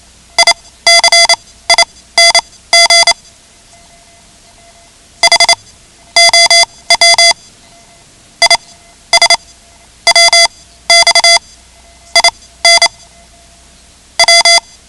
Morse code, heard as a keyed tone, about 28 wpm, spaced as for 15 wpm, 2050 Hz. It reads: ICING HOW IS WX IN W
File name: ham radio / CodeQuick / Code Quick 10